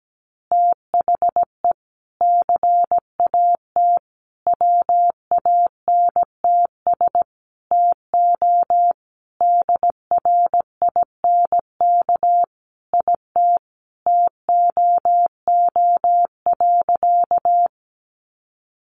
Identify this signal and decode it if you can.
Morse code, heard as a keyed tone, about 17 wpm, 700 Hz. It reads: THE CAT WANTS TO DRINK IT TOO.
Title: Additional Easy Phrases 17wpm